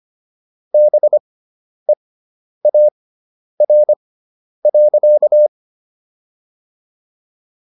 Morse code, heard as a keyed tone, about 25 wpm, spaced as for 10 wpm, 600 Hz. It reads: BEAR.